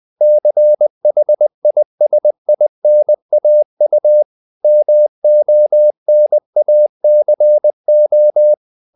Morse code, heard as a keyed tone, about 20 wpm, 600 Hz. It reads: CHISINAU MONACO
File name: CW / Practice / NationalCapitals20